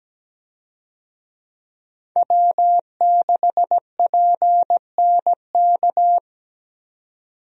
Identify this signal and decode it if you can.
Morse code, heard as a keyed tone, about 17 wpm, 700 Hz. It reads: W6PNK
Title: USA Call Signs - 5 Characters - 17wpm